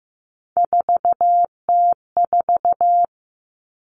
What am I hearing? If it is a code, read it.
Morse code, heard as a keyed tone, about 15 wpm, 700 Hz. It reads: 4T4
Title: Common Cut Numbers - 3-Digit 15wpm